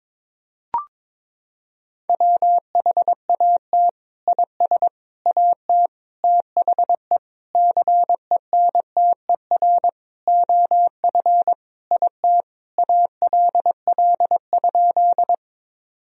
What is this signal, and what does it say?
Morse code, heard as a keyed tone, about 22 wpm, 700 Hz. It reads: WHAT IS AT THE CENTER OF IT ALL?
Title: Sentences from Top 1000 Words - Review Part 2 - 22wpm